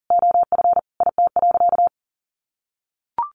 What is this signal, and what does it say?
Morse code, heard as a keyed tone, about 40 wpm, 700 Hz. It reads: OF IT.
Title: Sentences from Top 1400 Words 40wpm